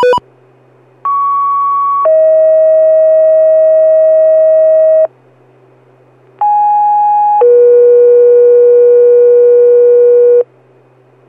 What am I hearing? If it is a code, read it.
Two-tone page, heard as a one-second tone, then a three-second tone, about 1120 then 640 Hz, about 830 then 480 Hz. It is heard from 1.0 to 10.4 s.